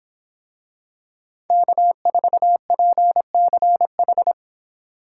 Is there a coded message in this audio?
Yes: Morse code from 1.5 to 4.3 s, heard as a keyed tone, about 24 wpm, 700 Hz. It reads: K4PCH